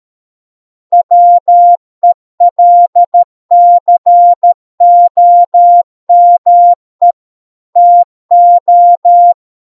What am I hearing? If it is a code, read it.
Morse code, heard as a keyed tone, about 13 wpm, 700 Hz. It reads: WELCOME TO